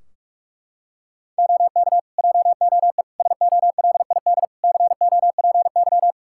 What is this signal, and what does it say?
Morse code, heard as a keyed tone, about 42 wpm, 700 Hz. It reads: OK JOE SOLID COPY
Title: K1USN SST QSO Practice 45wpm